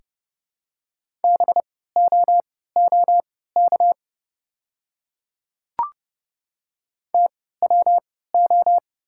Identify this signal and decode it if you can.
Morse code, heard as a keyed tone, about 30 wpm, spaced as for 17 wpm, 700 Hz. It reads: BOOK TWO
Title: 3x Character Spacing - Top 300 Words - 30wpm